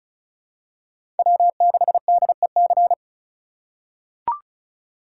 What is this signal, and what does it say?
Morse code, heard as a keyed tone, about 36 wpm, 700 Hz. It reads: W6DEC
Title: USA Call Signs - 6x Character Spacing - 35wpm